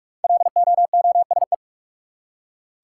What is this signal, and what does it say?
Morse code, heard as a keyed tone, about 46 wpm, 700 Hz. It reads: LOOSE
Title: Five Letter Words - 45wpm